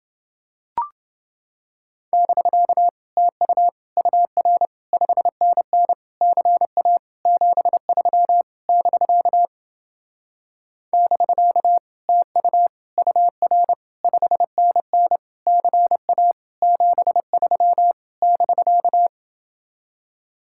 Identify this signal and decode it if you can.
Morse code, heard as a keyed tone, about 30 wpm, 700 Hz. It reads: <BK> TU UR 5NN CA 73 <BK> <BK> TU UR 5NN CA 73 <BK>